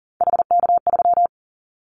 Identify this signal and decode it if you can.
Morse code, heard as a keyed tone, about 40 wpm, 700 Hz. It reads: HK3
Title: Callsign Prefixes for Top 100 DXCC - 40wpm